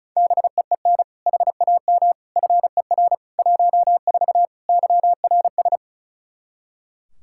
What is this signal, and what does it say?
Morse code, heard as a keyed tone, about 34 wpm, 700 Hz. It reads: BEEN HAM FER 14 YRS
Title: Practice QSO - Exchange 3 - More Info - 35wpm